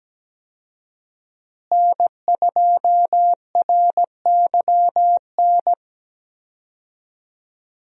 Morse code, heard as a keyed tone, about 17 wpm, 700 Hz. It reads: N2RYN
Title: USA Call Signs - 5 Characters - Repeated 3x - 17wpm